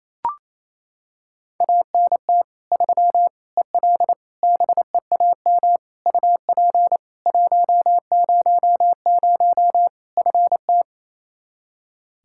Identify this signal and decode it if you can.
Morse code, heard as a keyed tone, about 28 wpm, 700 Hz. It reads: ANT 3 EL BEAM UP 100 FT
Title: Practice QSO - Exchange 2 - Info - 28wpm